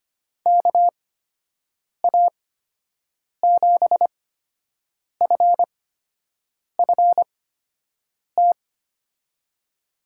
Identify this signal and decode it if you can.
Morse code, heard as a keyed tone, about 25 wpm, spaced as for 7 wpm, 700 Hz. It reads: KA7FFT
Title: USA Call Signs - 6 Characters - 8x Character Spacing - 25wpm